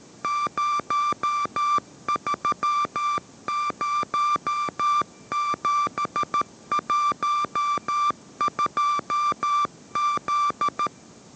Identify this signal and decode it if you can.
Morse code, heard as a keyed tone, about 13 wpm, 1180 Hz. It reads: 030712Z
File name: 14wpm